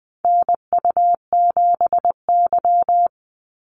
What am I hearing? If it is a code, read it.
Morse code, heard as a keyed tone, about 20 wpm, 700 Hz. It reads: NU7Y
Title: NA SOTA Activator Callsigns 20wpm